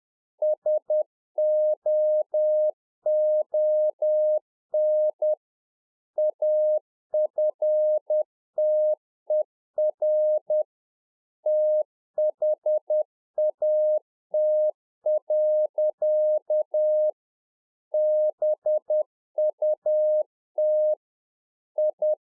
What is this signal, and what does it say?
Morse code, heard as a keyed tone, about 10 wpm, 610 Hz. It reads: SOON AFTER THAT. BUT I
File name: 2011-04-mp3 / A SMALL TRANCEIVER 1 10WPM